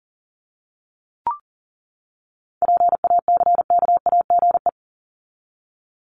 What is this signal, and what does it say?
Morse code, heard as a keyed tone, about 40 wpm, 700 Hz. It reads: PACKAGE
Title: Top 1501-1750 Words 40wpm